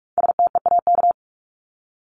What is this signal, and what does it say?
Morse code, heard as a keyed tone, about 44 wpm, 700 Hz. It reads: STEAK
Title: Top 1251-1500 Words 45wpm